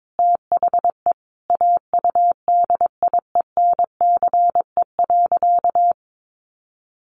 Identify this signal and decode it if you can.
Morse code, heard as a keyed tone, about 22 wpm, 700 Hz. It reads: THE AUDIENCE.
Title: Sentences from Top 800 Words 22wpm